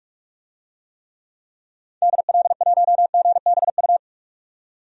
Morse code, heard as a keyed tone, about 45 wpm, 700 Hz. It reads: DL1GBU
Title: DXCC-100 Call Signs - 6 Characters - 45wpm